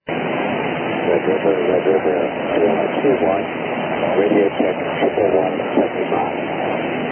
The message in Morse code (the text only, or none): none